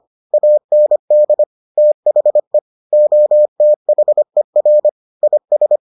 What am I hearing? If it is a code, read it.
Morse code, heard as a keyed tone, about 25 wpm, 600 Hz. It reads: AND THE OTHER IS